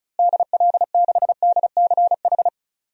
Morse code, heard as a keyed tone, about 34 wpm, 700 Hz. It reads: DL6DCH
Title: DXCC-100 Call Signs - 6 Characters - Repeated 3x - 35wpm